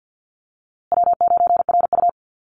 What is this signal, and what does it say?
Morse code, heard as a keyed tone, about 50 wpm, 700 Hz. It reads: W9RU
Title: USA Call Signs Repeated 2x 50wpm